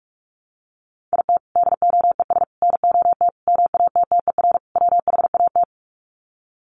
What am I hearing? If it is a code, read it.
Morse code, heard as a keyed tone, about 46 wpm, 700 Hz. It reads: IT DOES NOT MATTER WHAT
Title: Sets of 5 Words Encore 45wpm